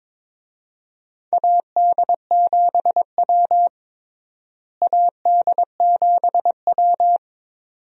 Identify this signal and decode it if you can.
Morse code, heard as a keyed tone, about 22 wpm, 700 Hz. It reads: AD7W AD7W
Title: USA Call Signs - 4 Characters - Repeated 2x - 22wpm